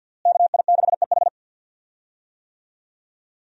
Morse code, heard as a keyed tone, about 52 wpm, 700 Hz. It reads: KI6EH
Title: USA Call Signs - 5 Characters - Repeated 3x - 50wpm